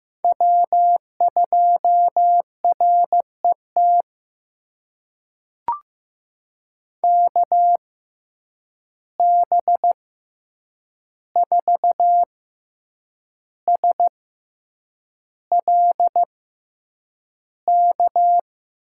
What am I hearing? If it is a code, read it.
Morse code, heard as a keyed tone, about 15 wpm, spaced as for 7 wpm, 700 Hz. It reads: W2RET KB4SLK
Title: USA Call Signs - 6x Character Spacing - 15wpm